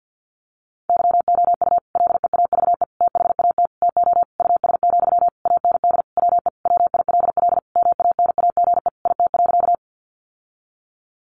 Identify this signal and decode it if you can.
Morse code, heard as a keyed tone, about 50 wpm, 700 Hz. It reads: YOU LEAVE THAT TO US, AND WE WILL MANAGE IT.
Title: Sentences from Top 1400 Words 50wpm